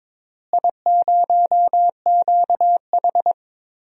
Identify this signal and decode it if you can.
Morse code, heard as a keyed tone, about 22 wpm, 700 Hz. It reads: I0QH
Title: DXCC-100 Call Signs - 4 Characters - Repeated 3x - 22wpm